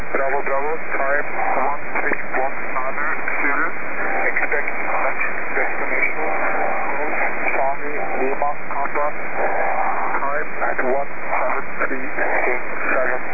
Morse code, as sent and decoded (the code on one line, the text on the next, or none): none